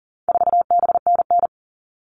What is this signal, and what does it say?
Morse code, heard as a keyed tone, about 38 wpm, 700 Hz. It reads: 4BNN
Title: All Cut Numbers - 4-Digit 40wpm